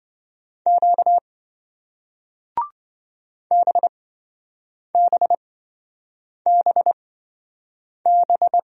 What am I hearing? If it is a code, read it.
Morse code, heard as a keyed tone, about 28 wpm, spaced as for 7 wpm, 700 Hz. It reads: Q BBBB